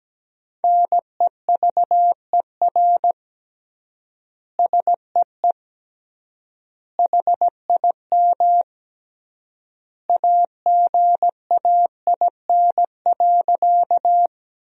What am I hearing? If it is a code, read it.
Morse code, heard as a keyed tone, about 17 wpm, 700 Hz. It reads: NEVER SEE HIM AGAIN.
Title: 3x Word-Spacing - Sentences from Top 300 Words - 17wpm